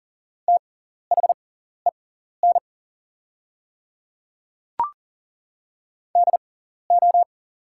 Morse code, heard as a keyed tone, about 40 wpm, spaced as for 14 wpm, 690 Hz. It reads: THEN DO
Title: 6x Character Spacing - Top 500 Words - 40wpm